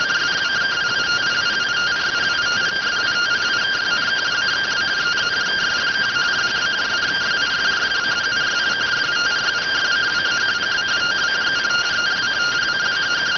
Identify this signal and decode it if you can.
Two-tone teleprinter signal, about 170 Hz shift.